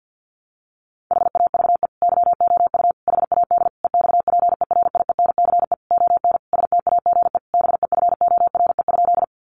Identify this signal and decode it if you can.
Morse code, heard as a keyed tone, about 50 wpm, 700 Hz. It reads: HAVE YOU HAD EXPERIENCE ON STAGE BEFORE?
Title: Sentences from Top 1200 Words 50wpm